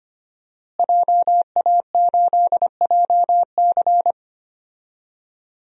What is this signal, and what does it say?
Morse code, heard as a keyed tone, about 25 wpm, 700 Hz. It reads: JA8JC
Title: DXCC-100 Call Signs - 5 Characters - Repeated 2x - 25wpm